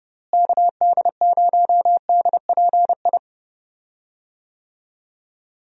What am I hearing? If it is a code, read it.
Morse code, heard as a keyed tone, about 30 wpm, 700 Hz. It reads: KD0DPI
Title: USA Call Signs - 6 Characters - Repeated 2x - 30wpm